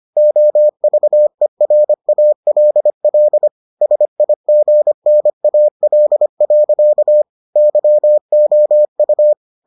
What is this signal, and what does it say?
Morse code, heard as a keyed tone, about 25 wpm, 600 Hz. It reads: OVERALL SIGNAL. YOU